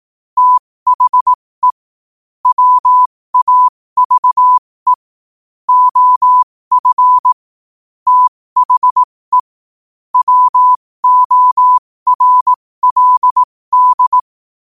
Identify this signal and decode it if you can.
Morse code, heard as a keyed tone, about 18 wpm, spaced as for 15 wpm, 1000 Hz. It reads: THE WAVE OF THE WORLD
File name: qotdfiles / 15wpm